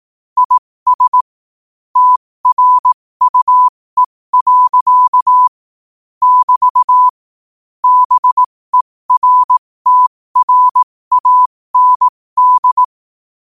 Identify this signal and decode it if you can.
Morse code, heard as a keyed tone, about 18 wpm, spaced as for 15 wpm, 1000 Hz. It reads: IS TRUE. <BT> BERTRAND
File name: qotdfiles / 15wpm